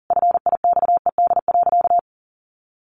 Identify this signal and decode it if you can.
Morse code, heard as a keyed tone, about 40 wpm, 700 Hz. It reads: FIXED.
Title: Sentences from Top 1100 Words 40wpm